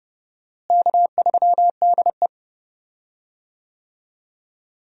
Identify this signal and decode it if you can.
Morse code, heard as a keyed tone, about 30 wpm, 700 Hz. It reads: K3DE
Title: USA Call Signs - 4 Characters - Repeated 2x - 30wpm